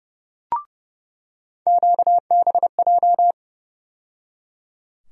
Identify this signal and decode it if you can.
Morse code, heard as a keyed tone, about 29 wpm, 700 Hz. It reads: QBJ
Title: USA Call Sign Postfixes - 3 Letters 30wpm